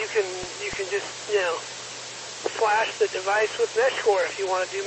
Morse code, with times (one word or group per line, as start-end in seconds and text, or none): none